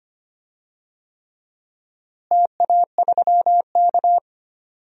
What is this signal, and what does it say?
Morse code, heard as a keyed tone, about 25 wpm, 700 Hz. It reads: TA3K
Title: DXCC-100 Call Signs - 4 Characters - Repeated 3x - 25wpm